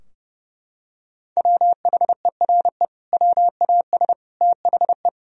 Morse code, heard as a keyed tone, about 30 wpm, 700 Hz. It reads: WHERE WAS THE
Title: Sentences from Top 1000 Words - Review Part 3 - 30wpm